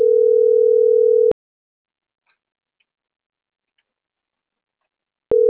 Ringback tone (North America), 440 + 480 Hz.